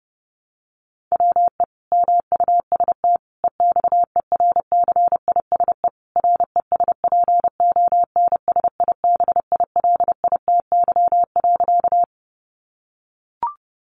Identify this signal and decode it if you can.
Morse code, heard as a keyed tone, about 30 wpm, 700 Hz. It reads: WE MUST EXERCISE RESPONSIBILITY.
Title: Sentences from Top 1200 Words 30wpm